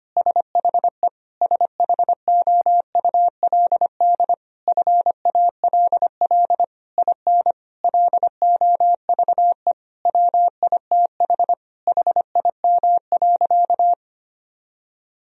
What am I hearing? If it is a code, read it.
Morse code, heard as a keyed tone, about 25 wpm, 700 Hz. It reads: SHE SHOULD FALL IN LOVE WITH HIM.
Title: Sentences from Top 1000 Words - Review Part 4 - 25wpm